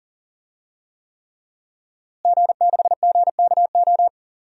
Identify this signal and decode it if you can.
Morse code, heard as a keyed tone, about 37 wpm, 700 Hz. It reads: G6GKO
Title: DXCC-100 Call Signs - 5 Characters - Repeated 3x - 40wpm